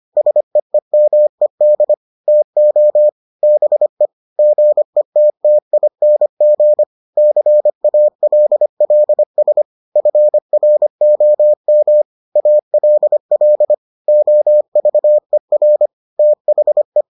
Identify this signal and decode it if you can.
Morse code, heard as a keyed tone, about 25 wpm, 600 Hz. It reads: SEEMED TO BE GETTING CALLS FROM ALL OVER THE